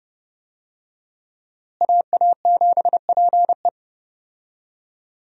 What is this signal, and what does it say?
Morse code, heard as a keyed tone, about 32 wpm, 700 Hz. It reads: AA7PE
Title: USA Call Signs - 5 Characters - Repeated 3x - 30wpm